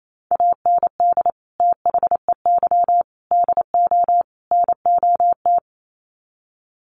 Morse code, heard as a keyed tone, about 28 wpm, 700 Hz. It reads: AND THEY DO NOT
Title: Sets of 4 Words 28wpm